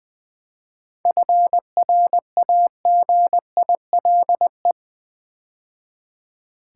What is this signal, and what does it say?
Morse code, heard as a keyed tone, about 20 wpm, 700 Hz. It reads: FRAGILE
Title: 7 Letter Words - First 3 Letters Given - 20wpm